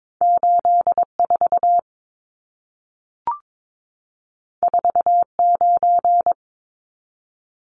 Morse code, heard as a keyed tone, about 22 wpm, 700 Hz. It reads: 84 49
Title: Numbers - 2-Digit 22wpm